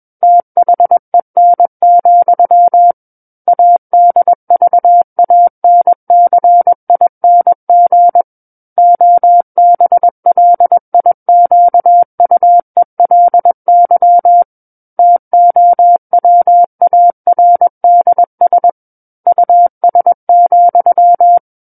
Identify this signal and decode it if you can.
Morse code, heard as a keyed tone, about 21 wpm, 700 Hz. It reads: THEN, ADVANCING OBLIQUELY TOWARDS US,
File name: Book One / 12-Chapter 12 - 21 WPM